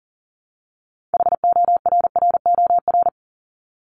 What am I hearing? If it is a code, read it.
Morse code, heard as a keyed tone, about 39 wpm, 700 Hz. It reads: HORROR
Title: Top 1501-1750 Words 40wpm